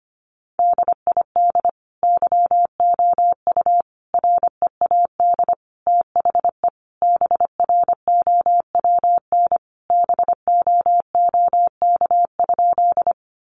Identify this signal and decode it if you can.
Morse code, heard as a keyed tone, about 25 wpm, 700 Hz. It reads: DID YOU READ THE BROWN BOOK?
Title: Sentences from Top 1200 Words 25wpm